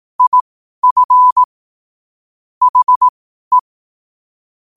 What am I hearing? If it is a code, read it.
Morse code, heard as a keyed tone, about 18 wpm, spaced as for 12 wpm, 1000 Hz. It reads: IF HE
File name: qotdfiles / 12wpm